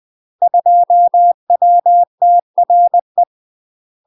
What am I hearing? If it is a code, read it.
Morse code, heard as a keyed tone, about 20 wpm, 700 Hz. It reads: 2WTRE
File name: UK MASTER QSOs TRAINING / 20WPM / UK MASTER QSOs 0047